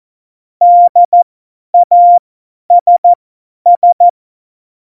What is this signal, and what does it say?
Morse code, heard as a keyed tone, about 14 wpm, spaced as for 10 wpm, 700 Hz. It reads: DASS
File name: Morsekurs / Witze / Witz 06. Oktober 2020, 50 F70